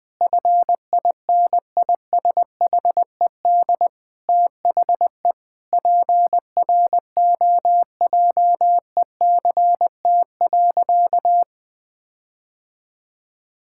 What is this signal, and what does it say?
Morse code, heard as a keyed tone, about 20 wpm, 700 Hz. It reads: FINISHED THE PROJECT.
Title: Sentences with 4-Word Phrases - 20wpm